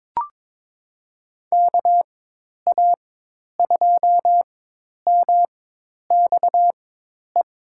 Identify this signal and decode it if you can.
Morse code, heard as a keyed tone, about 22 wpm, spaced as for 10 wpm, 700 Hz. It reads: KA2MXE
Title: USA Call Signs - 6 Characters - 4x Character Spacing - 22wpm